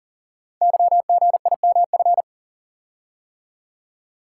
Morse code, heard as a keyed tone, about 42 wpm, 700 Hz. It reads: YGIMF
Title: Code Groups - 5 Letter:Numbers - 40wpm